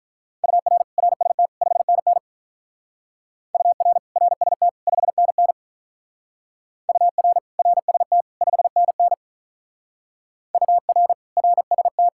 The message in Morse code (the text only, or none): UR RST 5NN UR RST 5NN UR RST 5NN UR RST